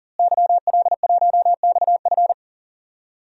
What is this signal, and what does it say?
Morse code, heard as a keyed tone, about 40 wpm, 700 Hz. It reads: YL1XF
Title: DXCC-100 Call Signs - 5 Characters - Repeated 3x - 40wpm